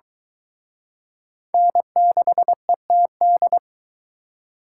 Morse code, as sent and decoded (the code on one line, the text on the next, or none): -. -.... - -..
N6ETD